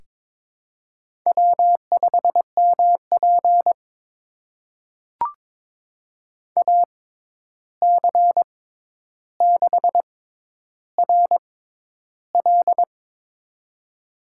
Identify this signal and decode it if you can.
Morse code, heard as a keyed tone, about 22 wpm, spaced as for 11 wpm, 700 Hz. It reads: W5MP AC6RL
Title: USA Call Signs - 6x Character Spacing - 22wpm